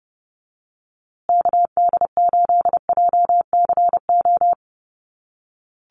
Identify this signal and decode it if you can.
Morse code, heard as a keyed tone, about 30 wpm, 700 Hz. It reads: KD8JCO
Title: USA Call Signs Repeated 3x 30wpm